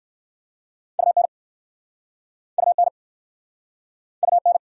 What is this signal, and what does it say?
Morse code, heard as a keyed tone, about 60 wpm, 700 Hz. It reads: UN UN UN